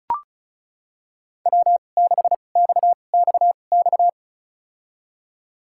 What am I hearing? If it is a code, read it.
Morse code, heard as a keyed tone, about 34 wpm, spaced as for 25 wpm, 700 Hz. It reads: W6XXX
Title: USA Call Signs - 5 Characters - 2x Character Spacing - 35wpm